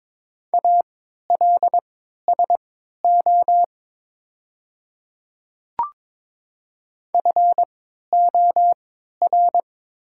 Morse code, heard as a keyed tone, about 22 wpm, spaced as for 13 wpm, 700 Hz. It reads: ALSO FOR
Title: 3x Character Spacing - Top 300 Words - 22wpm